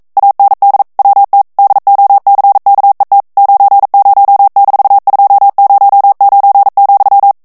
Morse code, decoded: AND WT DOCKET 90-209,